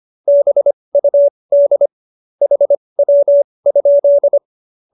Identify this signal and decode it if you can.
Morse code, heard as a keyed tone, about 25 wpm, spaced as for 20 wpm, 570 Hz. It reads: BUD HW?
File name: QSO302 20